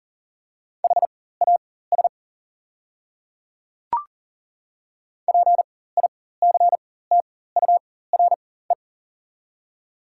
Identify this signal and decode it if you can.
Morse code, heard as a keyed tone, about 37 wpm, spaced as for 18 wpm, 690 Hz. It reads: HAS PICTURE
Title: 4x Character Spacing - Top 200 Words - 40wpm